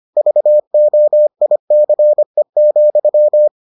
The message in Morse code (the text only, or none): VOICE,